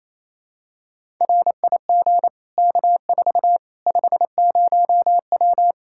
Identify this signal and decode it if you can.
Morse code, heard as a keyed tone, about 27 wpm, 700 Hz. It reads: RIG K4 50W
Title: Practice QSO - Exchange 2 - Info - 28wpm